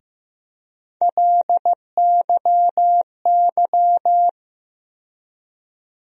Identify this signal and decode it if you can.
Morse code, heard as a keyed tone, about 15 wpm, 700 Hz. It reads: LYY
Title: USA Call Sign Postfixes - 3 Letters 15wpm